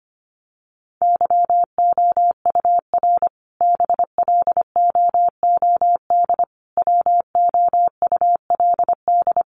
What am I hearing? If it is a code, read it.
Morse code, heard as a keyed tone, about 25 wpm, 700 Hz. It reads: YOUR BLOOD WOULD
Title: Sentences from Top 1100 Words 25wpm